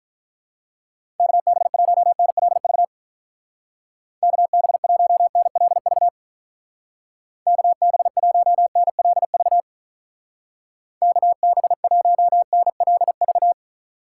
Morse code, heard as a keyed tone, about 42 wpm, 700 Hz. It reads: KB1NLV KB1NLV KB1NLV KB1NLV
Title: Speed Racing - USA Call Signs 35wpm